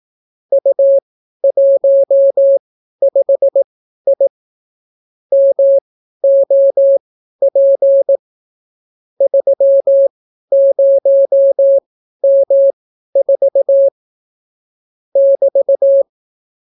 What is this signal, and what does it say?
Morse code, heard as a keyed tone, about 18 wpm, spaced as for 12 wpm, 550 Hz. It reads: U15I MOP 30M4 <BT>